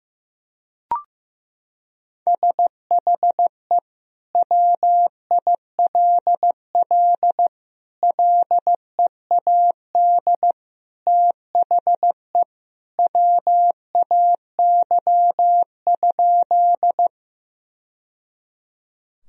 Morse code, heard as a keyed tone, about 15 wpm, 700 Hz. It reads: SHE WILL LEAD THE WAY?
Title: Sentences from Top 1000 Words - Review Part 2 - 15wpm